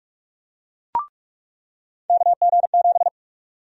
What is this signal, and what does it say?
Morse code, heard as a keyed tone, about 48 wpm, 700 Hz. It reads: KG7
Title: USA Call Sign Prefixes - 45wpm